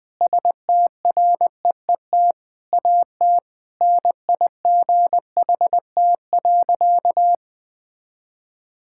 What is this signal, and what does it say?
Morse code, heard as a keyed tone, about 20 wpm, 700 Hz. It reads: STREET AT NIGHT.
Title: Sentences from Top 1000 Words - Review Part 6 - 20wpm